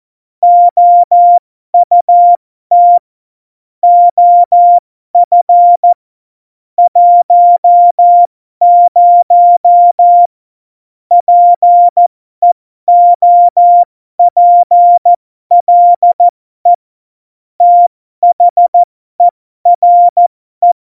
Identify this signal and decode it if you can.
Morse code, heard as a keyed tone, about 14 wpm, spaced as for 12 wpm, 700 Hz. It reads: OUT OF 10 PEOPLE THERE
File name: Morsekurs / Witze